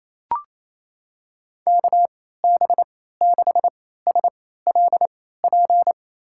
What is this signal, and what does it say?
Morse code, heard as a keyed tone, about 28 wpm, spaced as for 16 wpm, 700 Hz. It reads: KB6SLP
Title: USA Call Signs - 6 Characters - 3x Character Spacing - 28wpm